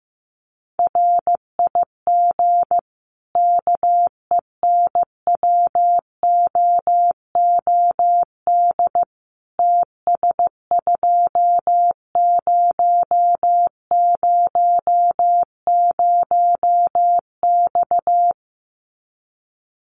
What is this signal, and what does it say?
Morse code, heard as a keyed tone, about 15 wpm, 700 Hz. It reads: RIG KENWOOD TS2000X
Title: Common QSO Phrases 15wpm